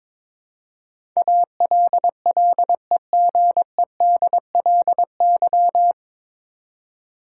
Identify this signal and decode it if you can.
Morse code, heard as a keyed tone, about 22 wpm, 700 Hz. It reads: ALLEGEDLY
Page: Morse Code Ninja practice 22wpm